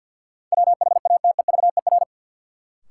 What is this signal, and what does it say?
Morse code, heard as a keyed tone, about 53 wpm, 700 Hz. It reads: WHATEVER
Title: Sentences from Top 1100 Words - New Word Drill - 50wpm